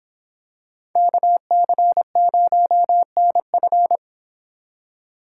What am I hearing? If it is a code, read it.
Morse code, heard as a keyed tone, about 23 wpm, 700 Hz. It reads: KC0NF